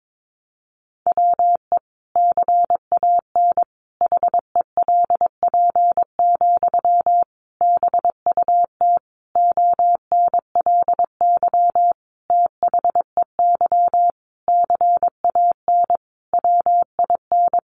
Morse code, heard as a keyed tone, about 22 wpm, 700 Hz. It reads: WE CAN HELP, BUT ONLY THEY CAN WIN
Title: Sentences from Top 1500 Words 22wpm